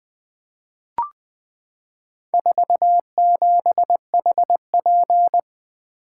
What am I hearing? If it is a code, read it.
Morse code, heard as a keyed tone, about 20 wpm, 700 Hz. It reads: 47HP
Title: Four Letter-Number Combinations - 20wpm